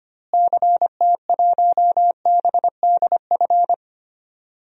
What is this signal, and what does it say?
Morse code, heard as a keyed tone, about 25 wpm, 700 Hz. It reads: CT1BDF
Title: DXCC-100 Call Signs - 6 Characters - Repeated 3x - 25wpm